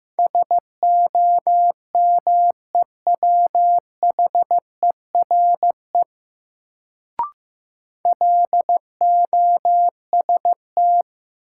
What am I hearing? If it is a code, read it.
Morse code, heard as a keyed tone, about 15 wpm, 700 Hz. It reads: SOMEWHERE LOST
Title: Sentences from Top 1400 Words - New Word Drill - 15wpm